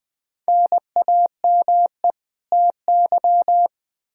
Morse code, heard as a keyed tone, about 20 wpm, 700 Hz. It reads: NAME TY
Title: Practice QSO - Exchange 1 - Beginning - 20wpm